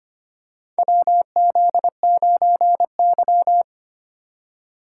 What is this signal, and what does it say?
Morse code, heard as a keyed tone, about 25 wpm, 700 Hz. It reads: WZ9Y
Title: USA Call Signs - 4 Characters - 8x Character Spacing - 25wpm